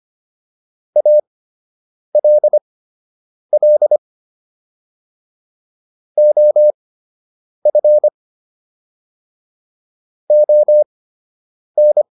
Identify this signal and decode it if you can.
Morse code, heard as a keyed tone, about 25 wpm, spaced as for 8 wpm, 600 Hz. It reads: ALL OF ON
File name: WD104 08